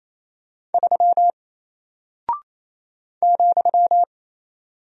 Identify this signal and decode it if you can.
Morse code, heard as a keyed tone, about 28 wpm, 700 Hz. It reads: 3 ,
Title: Single Character (Letter:Number:Punctuation:Prosign) Alt 28wpm